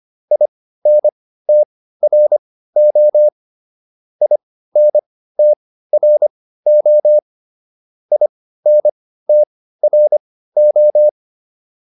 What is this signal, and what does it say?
Morse code, heard as a keyed tone, about 25 wpm, spaced as for 15 wpm, 600 Hz. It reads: INTRO INTRO INTRO
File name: IN5 15